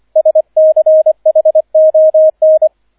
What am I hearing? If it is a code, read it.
Morse code, heard as a keyed tone, about 25 wpm, 630 Hz, with radio noise. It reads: SCHON